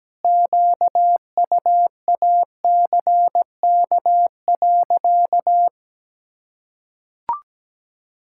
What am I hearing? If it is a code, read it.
Morse code, heard as a keyed tone, about 17 wpm, 700 Hz. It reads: QUACK.
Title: Pangram Challenge v2 - 17wpm